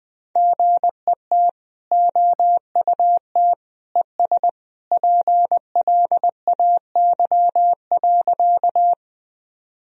Morse code, heard as a keyed tone, about 20 wpm, 700 Hz. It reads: GET OUT ES PLAY.
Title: Sentences from Top 200 Words and Abbreviations 20wpm